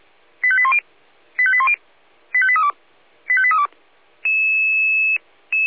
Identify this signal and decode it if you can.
Five-tone selective call (ZVEI), heard as quick runs of five tones, tones of about 70 ms. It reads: 85810 85810 85832 85832
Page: /496/SDR/ZVEI/85839/